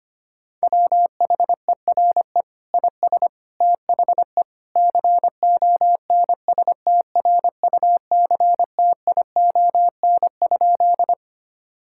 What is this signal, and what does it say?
Morse code, heard as a keyed tone, about 25 wpm, 700 Hz. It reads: WHERE IS THE CONSTRUCTION?
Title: Sentences from Top 1600 Words 25wpm